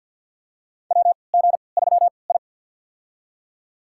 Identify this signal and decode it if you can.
Morse code, heard as a keyed tone, about 49 wpm, spaced as for 28 wpm, 700 Hz. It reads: WG3I
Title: USA Call Signs - 4 Characters - 3x Character Spacing - 50wpm